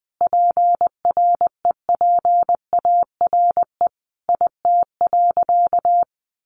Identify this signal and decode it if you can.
Morse code, heard as a keyed tone, about 20 wpm, 700 Hz. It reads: PREPARE IT.
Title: Sentences from Top 1100 Words 20wpm